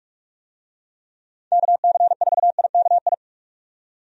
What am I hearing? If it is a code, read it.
Morse code, heard as a keyed tone, about 46 wpm, 700 Hz. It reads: KC4IKI
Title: USA Call Signs - 6 Characters - 45wpm